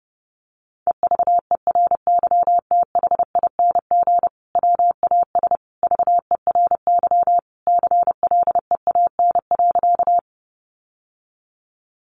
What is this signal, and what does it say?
Morse code, heard as a keyed tone, about 30 wpm, 700 Hz. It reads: EVERYTHING WAS VERY CLEAN.
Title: Sentences from Top 1100 Words 30wpm